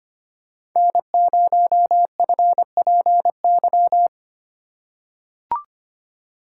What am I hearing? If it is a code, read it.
Morse code, heard as a keyed tone, about 25 wpm, 700 Hz. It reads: N0FPY
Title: USA Call Signs - 5 Characters - 3x Character Spacing - 25wpm